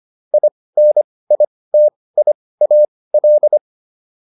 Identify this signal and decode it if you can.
Morse code, heard as a keyed tone, about 25 wpm, spaced as for 18 wpm, 600 Hz. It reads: INITIAL